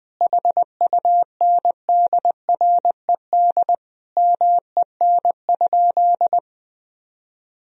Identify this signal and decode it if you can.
Morse code, heard as a keyed tone, about 21 wpm, 700 Hz. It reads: HUNDRED MEN?
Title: Sentences from Top 1000 Words - Review Part 2 - 20wpm